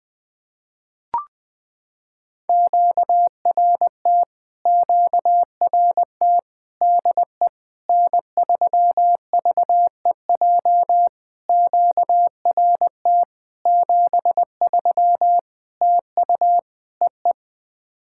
Morse code, heard as a keyed tone, about 20 wpm, 700 Hz. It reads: QRT QRT DE N3VEJ QRT 73 TU EE
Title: POTA QSO Practice - 20wpm